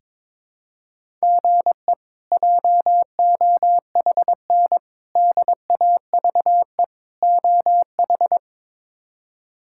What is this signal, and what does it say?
Morse code, heard as a keyed tone, about 22 wpm, 700 Hz. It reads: GE JOHN DAVE OH
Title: K1USN SST QSO Practice - 22wpm